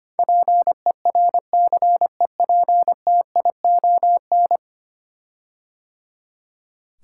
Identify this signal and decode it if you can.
Morse code, heard as a keyed tone, about 25 wpm, 700 Hz. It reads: PERCEPTION
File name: 10 Letter Words - First 3 Letters Given - 25wpm